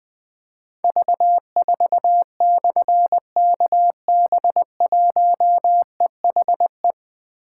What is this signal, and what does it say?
Morse code, heard as a keyed tone, about 20 wpm, 700 Hz. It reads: V4/KB1EHE
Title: Difficult Call Signs 20wpm